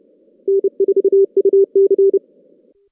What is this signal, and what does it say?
Morse code, heard as a keyed tone, about 33 wpm, 390 Hz. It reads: N4UC
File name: 2016 JAN CW NAQP / 2016 Jan naqp k5ot - 0-11